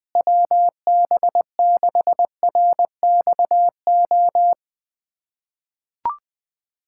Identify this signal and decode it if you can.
Morse code, heard as a keyed tone, about 21 wpm, 700 Hz. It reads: WB6RXO
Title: USA Callsign DE Callsign - 6 Characters - 20wpm